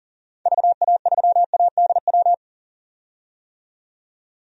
Morse code, heard as a keyed tone, about 40 wpm, 700 Hz. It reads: VA3ADW
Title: DXCC-100 Call Signs - 6 Characters - 40wpm